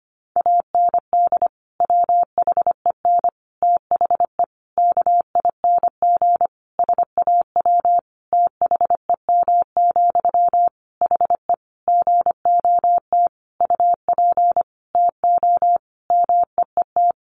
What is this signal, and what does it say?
Morse code, heard as a keyed tone, about 25 wpm, 700 Hz. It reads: AND WHEN THE KING SAW THEM, HE GOT UP TO MEET
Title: Sentences from Top 1100 Words 25wpm